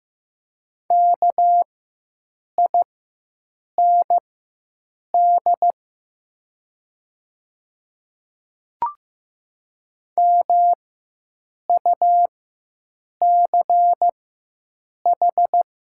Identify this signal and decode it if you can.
Morse code, heard as a keyed tone, about 15 wpm, spaced as for 7 wpm, 700 Hz. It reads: KIND MUCH
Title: 4x Character Spacing - Top 200 Words - 15wpm